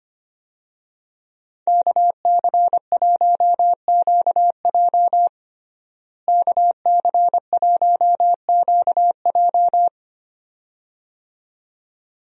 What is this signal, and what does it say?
Morse code, heard as a keyed tone, about 25 wpm, 700 Hz. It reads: KC1QJ KC1QJ
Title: USA Call Signs - 5 Characters - Repeated 2x - 25wpm